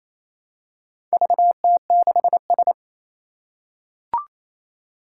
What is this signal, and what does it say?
Morse code, heard as a keyed tone, about 27 wpm, 700 Hz. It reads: VT6S
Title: Four Challenging Letter-Number Combinations 28wpm